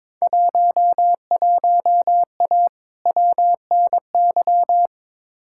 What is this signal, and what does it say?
Morse code, heard as a keyed tone, about 22 wpm, 700 Hz. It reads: 11A WNY
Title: Field Day Exchanges Encore - 22wpm